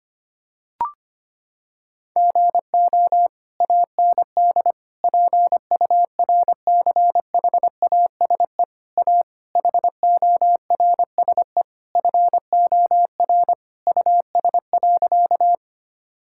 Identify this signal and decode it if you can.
Morse code, heard as a keyed tone, about 25 wpm, 700 Hz. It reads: GO AND PURCHASE A HORSE FOR US.
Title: Sentences from Top 1000 Words - Review Part 5 - 25wpm